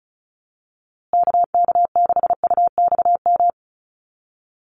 Morse code, heard as a keyed tone, about 34 wpm, 700 Hz. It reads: KK6UXM
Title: USA Call Signs Repeated 2x 35wpm